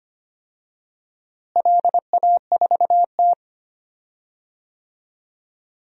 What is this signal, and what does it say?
Morse code, heard as a keyed tone, about 25 wpm, 700 Hz. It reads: LA4T
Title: DXCC-100 Call Signs - 4 Characters - Repeated 2x - 25wpm